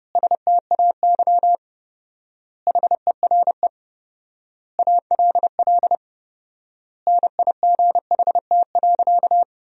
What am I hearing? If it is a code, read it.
Morse code, heard as a keyed tone, about 30 wpm, 700 Hz. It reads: STAY HERE ALL NIGHT.
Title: 4x Word-Spacing - Sentences from Top 500 Words - 30wpm